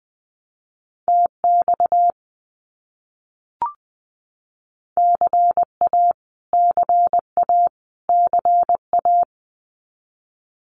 Morse code, heard as a keyed tone, about 20 wpm, 700 Hz. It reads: TX CA CA CA
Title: US States Repeated 3x 20wpm